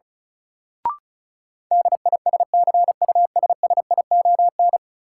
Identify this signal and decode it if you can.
Morse code, heard as a keyed tone, about 36 wpm, 700 Hz. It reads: DISCUSSION
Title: Sentences from Top 1100 Words - New Word Drill Alt - 35wpm